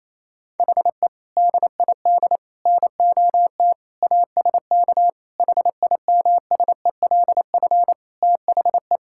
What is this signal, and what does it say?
Morse code, heard as a keyed tone, about 28 wpm, 700 Hz. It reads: HE DID NOT ASK HIMSELF THE
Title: Sentences from Top 1600 Words 28wpm